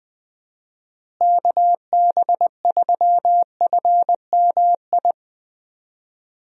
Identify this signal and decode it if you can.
Morse code, heard as a keyed tone, about 21 wpm, 700 Hz. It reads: KB3FMI
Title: USA Call Signs - 6 Characters - Repeated 2x - 20wpm